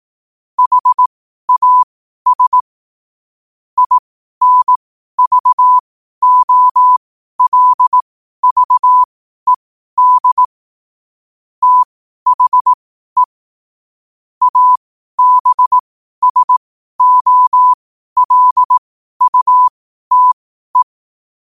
Morse code, heard as a keyed tone, about 18 wpm, spaced as for 12 wpm, 1000 Hz. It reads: HAS INVOLVED THE ABSOLUTE
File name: qotdfiles / 12wpm